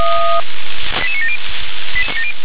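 Five-tone selective call (ZVEI), heard as quick runs of five tones, tones of about 70 ms. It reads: 88970
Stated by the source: ZVEI five-tone call 88778 received by SDR.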